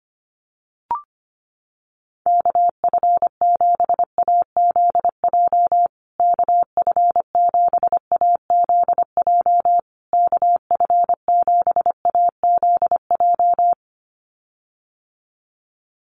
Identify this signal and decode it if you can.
Morse code, heard as a keyed tone, about 25 wpm, 700 Hz. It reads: KF7AZJ KF7AZJ KF7AZJ
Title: USA Call Signs Repeated 3x 25wpm